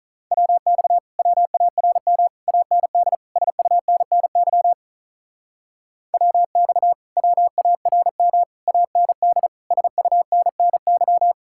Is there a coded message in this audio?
Yes: Morse code from 0.3 to 11.3 s, heard as a keyed tone, about 45 wpm, 700 Hz. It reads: WX WARM AND SUNNY WX WARM AND SUNNY